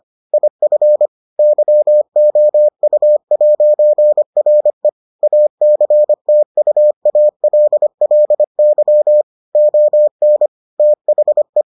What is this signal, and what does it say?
Morse code, heard as a keyed tone, about 25 wpm, 600 Hz. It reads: IF YOU'RE ACTUALLY ON THE